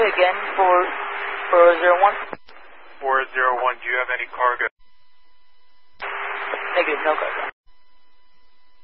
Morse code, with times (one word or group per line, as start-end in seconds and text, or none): none